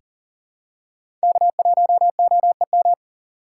Morse code, heard as a keyed tone, about 40 wpm, 700 Hz. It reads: K1OEM
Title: USA Call Signs - 5 Characters - 2x Character Spacing - 40wpm